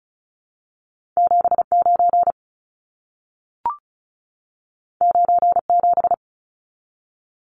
Morse code, heard as a keyed tone, about 36 wpm, 700 Hz. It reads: 79 97
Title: Numbers - 2-Digit 35wpm